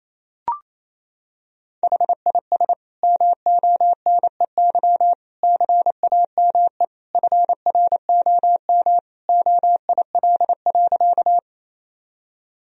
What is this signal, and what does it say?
Morse code, heard as a keyed tone, about 27 wpm, 700 Hz. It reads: HIS MONEY CAME FROM OIL.
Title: Sentences from Top 1000 Words - Review Part 4 - 28wpm